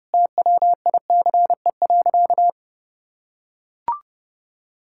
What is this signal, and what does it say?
Morse code, heard as a keyed tone, about 28 wpm, 700 Hz. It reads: TWICE.
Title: Sentences from Top 1600 Words 30wpm